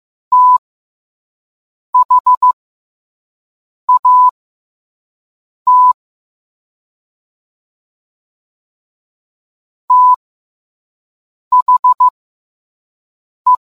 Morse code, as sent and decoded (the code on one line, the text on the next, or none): - .... .- - / - .... .
THAT THE